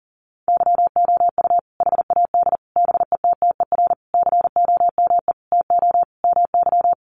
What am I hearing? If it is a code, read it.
Morse code, heard as a keyed tone, about 40 wpm, 700 Hz. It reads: YOU HAD BETTER COME TO MY
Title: Sentences from Top 400 Words 40wpm